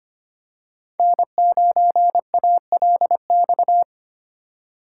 Morse code, heard as a keyed tone, about 25 wpm, 700 Hz. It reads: N9ALX
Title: USA Call Signs - 5 Characters - 2x Character Spacing - 25wpm